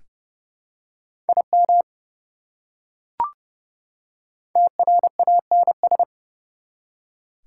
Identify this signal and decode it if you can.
Morse code, heard as a keyed tone, about 30 wpm, 700 Hz. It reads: IM TRANS
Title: Prefixes and Suffixes 30wpm